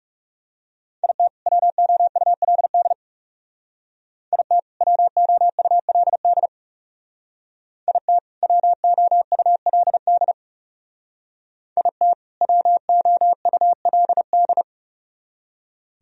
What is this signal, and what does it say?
Morse code, heard as a keyed tone, about 36 wpm, 700 Hz. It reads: IT WOULD IT WOULD IT WOULD IT WOULD
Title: Speed-Racing - Sets of 2 Words 30wpm